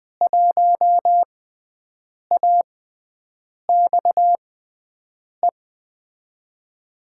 Morse code, heard as a keyed tone, about 20 wpm, spaced as for 7 wpm, 700 Hz. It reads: 1AXE
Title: USA Call Signs - 6 Characters - 6x Character Spacing - 20wpm